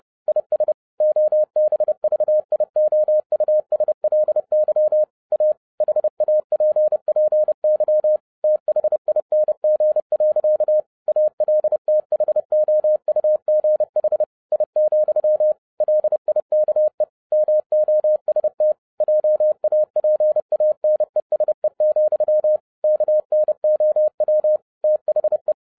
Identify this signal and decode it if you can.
Morse code, heard as a keyed tone, about 30 wpm, 610 Hz. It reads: IS OBVIOUSLY A HAPPY THING. ALTHOUGH I, LIKE MOST JAPANESE, KNOW THE
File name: 2011-07-mp3 / n60 BACK FROM A BIG ACCIDENT 30wpm